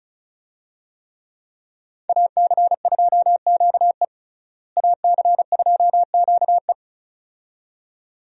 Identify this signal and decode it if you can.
Morse code, heard as a keyed tone, about 36 wpm, 700 Hz. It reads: AC2QE AC2QE
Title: USA Call Signs - 5 Characters - Repeated 2x - 35wpm